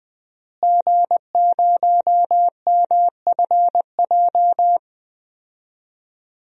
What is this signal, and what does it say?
Morse code, heard as a keyed tone, about 21 wpm, 700 Hz. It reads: G0MFJ
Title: DXCC-100 Call Signs - 5 Characters - Repeated 2x - 20wpm